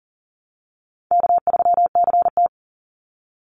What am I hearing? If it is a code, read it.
Morse code, heard as a keyed tone, about 37 wpm, 700 Hz. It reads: K3CT
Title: Call Signs Repeated 3x - 40wpm